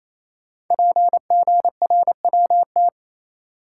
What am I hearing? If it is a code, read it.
Morse code, heard as a keyed tone, about 28 wpm, 700 Hz. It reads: PGRWT
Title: Code Groups - 5 Letters - 28wpm